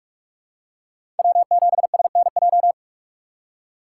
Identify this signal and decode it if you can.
Morse code, heard as a keyed tone, about 46 wpm, 700 Hz. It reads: W7SNJ
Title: USA Call Signs - 3x Character Spacing - 45wpm